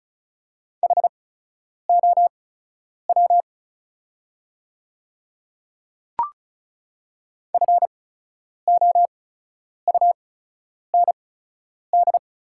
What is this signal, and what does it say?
Morse code, heard as a keyed tone, about 36 wpm, spaced as for 10 wpm, 690 Hz. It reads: HOW FOUND
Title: 8x Character Spacing - Top 300 Words - 35wpm